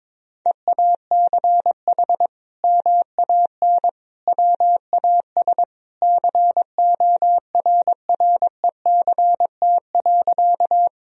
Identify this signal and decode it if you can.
Morse code, heard as a keyed tone, about 22 wpm, 700 Hz. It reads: EACH MAN WAS CORRECT.
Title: Sentences from Top 1000 Words - Review Part 2 - 22wpm